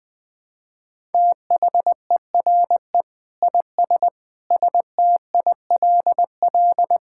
Morse code, heard as a keyed tone, about 21 wpm, 700 Hz. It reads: THERE IS STILL
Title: Sentences with 8-Word Phrases - Spoken Beginning - 20wpm